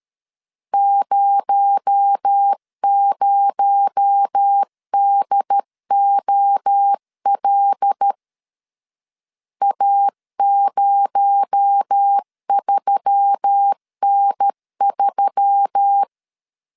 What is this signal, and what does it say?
Morse code, heard as a keyed tone, about 13 wpm, 800 Hz. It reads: 00DOL A03N3
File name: CW / DET / 1 6 1500C S C 12wpm 3533